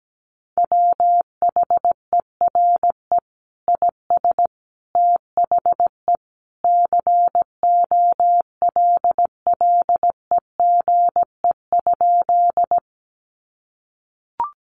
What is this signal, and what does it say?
Morse code, heard as a keyed tone, about 17 wpm, 700 Hz. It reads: WHERE IS THE COLLEGE?
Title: Sentences from Top 1300 Words 17wpm